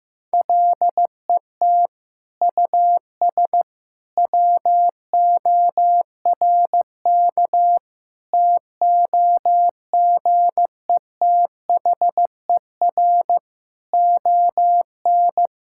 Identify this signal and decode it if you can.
Morse code, heard as a keyed tone, about 15 wpm, 700 Hz. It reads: LET US WORK TOGETHER ON
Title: Sentences from Top 1000 Words - Review Part 5 - 15wpm